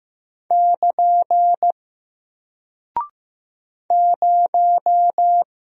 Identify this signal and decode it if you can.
Morse code, heard as a keyed tone, about 15 wpm, 700 Hz. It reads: <KN> 0
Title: Single Character (Letter:Number:Punctuation:Prosign) Alt 15wpm